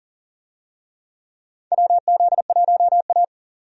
Morse code, heard as a keyed tone, about 37 wpm, 700 Hz. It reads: WZ1A
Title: USA Call Signs - 4 Characters - 40wpm